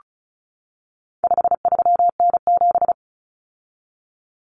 Morse code, heard as a keyed tone, about 34 wpm, 700 Hz. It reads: 53N7
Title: Common Cut Numbers - 4-Digit 35wpm